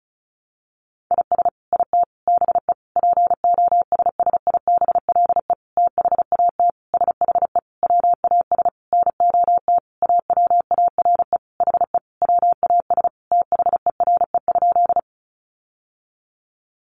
Morse code, heard as a keyed tone, about 34 wpm, 700 Hz. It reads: IS IT BE POSSIBLE THAT SHE WAS NOT AWARE HE WAS THERE?
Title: Sentences from Top 1200 Words 35wpm